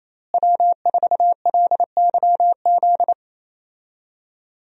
Morse code, heard as a keyed tone, about 28 wpm, 700 Hz. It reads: W4LYZ
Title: USA Call Signs - 5 Characters - Repeated 3x - 28wpm